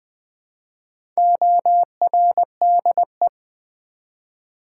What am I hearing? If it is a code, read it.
Morse code, heard as a keyed tone, about 20 wpm, 700 Hz. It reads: ORDE
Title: Build A Word - 1 Letter At A Time - 5-Letter Words - 20wpm